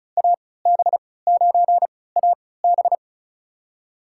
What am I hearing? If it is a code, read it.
Morse code, heard as a keyed tone, about 36 wpm, spaced as for 20 wpm, 700 Hz. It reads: AB9AB
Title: USA Call Signs - 5 Characters - 3x Character Spacing - 35wpm